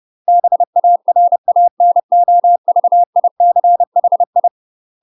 Morse code, heard as a keyed tone, about 30 wpm, 700 Hz. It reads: BARANOVICHI
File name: UK MASTER QSOs TRAINING / 30WPM / UK MASTER QSOs 0010